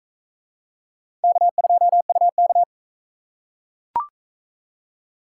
Morse code, heard as a keyed tone, about 42 wpm, 700 Hz. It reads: K2UK